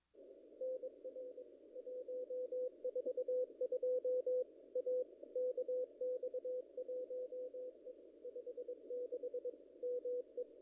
Morse code, heard as a keyed tone, about 22 wpm, 500 Hz. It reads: NR 142 A KX1E 56 ME
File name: contests / ss-cw-k5tr-op-n5rz -0100